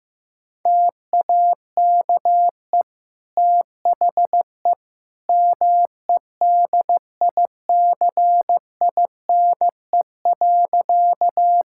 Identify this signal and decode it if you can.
Morse code, heard as a keyed tone, about 16 wpm, 700 Hz. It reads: TAKE THE MEDICINE.
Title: Sentences from Top 1000 Words - Review Part 6 - 15wpm